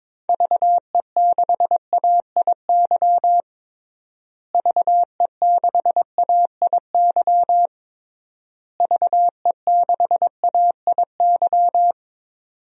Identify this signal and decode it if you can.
Morse code, heard as a keyed tone, about 22 wpm, 700 Hz. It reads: VE6AIY VE6AIY VE6AIY
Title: DXCC-100 Call Signs - 6 Characters - Repeated 3x - 22wpm